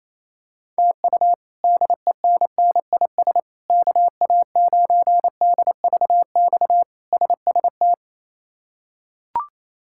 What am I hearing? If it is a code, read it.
Morse code, heard as a keyed tone, about 28 wpm, 700 Hz. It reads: TU DENNIS KA9DVX SST
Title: K1USN SST QSO Practice - 28wpm